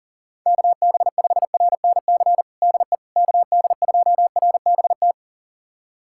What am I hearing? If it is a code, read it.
Morse code, heard as a keyed tone, about 40 wpm, 690 Hz. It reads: KB5RNC DE KD2RBT
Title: USA Callsign DE Callsign - 6 Characters - 40wpm